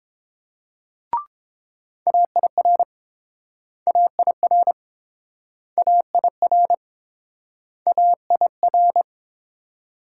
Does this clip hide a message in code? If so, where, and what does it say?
Yes: Morse code from 2.1 to 9.0 s, heard as a keyed tone, about 31 wpm, 700 Hz. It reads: AIR AIR AIR AIR